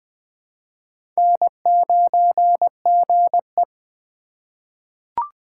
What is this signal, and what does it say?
Morse code, heard as a keyed tone, about 21 wpm, 700 Hz. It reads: N9GE
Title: USA Call Signs - 4x Character Spacing - 20wpm